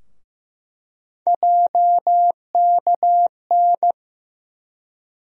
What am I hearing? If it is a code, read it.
Morse code, heard as a keyed tone, about 15 wpm, 700 Hz. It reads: JKN
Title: USA Call Sign Postfixes - 3 Letters 15wpm